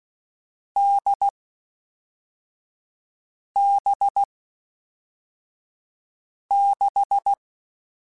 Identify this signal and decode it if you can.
Morse code, heard as a keyed tone, about 16 wpm, 800 Hz. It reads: D B 6
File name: Difficult Combos / Difficult Combos-mp3s / NDB NDB6 @5